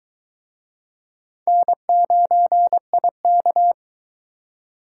Morse code, heard as a keyed tone, about 23 wpm, 700 Hz. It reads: N9IK